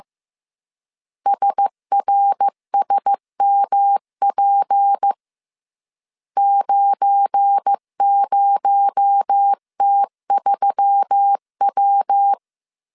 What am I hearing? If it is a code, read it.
Morse code, heard as a keyed tone, about 15 wpm, 800 Hz. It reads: SRSMP 90T3W
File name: CW / DET / 1 9 1500C S A 14wpm 3222